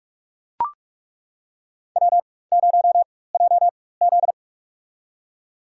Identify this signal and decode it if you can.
Morse code, heard as a keyed tone, about 42 wpm, spaced as for 21 wpm, 700 Hz. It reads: W0JZ
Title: USA Call Signs - 4 Characters - 4x Character Spacing - 45wpm